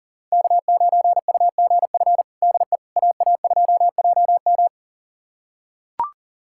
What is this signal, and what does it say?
Morse code, heard as a keyed tone, about 40 wpm, 700 Hz. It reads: K9UGF DE AA2JM
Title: USA Callsign DE Callsign - 5 Characters - 40wpm